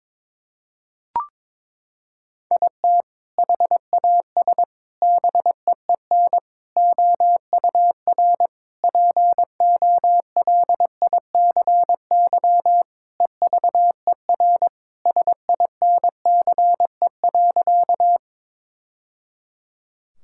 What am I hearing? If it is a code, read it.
Morse code, heard as a keyed tone, about 22 wpm, 700 Hz. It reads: IT HAS BEEN OUR POLICY EVER SINCE.
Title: Sentences from Top 1000 Words - Review Part 5 - 22wpm